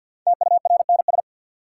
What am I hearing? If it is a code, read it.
Morse code, heard as a keyed tone, about 52 wpm, 700 Hz. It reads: TURNS
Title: Sets of 5 Words Encore - New Word Drill - 50wpm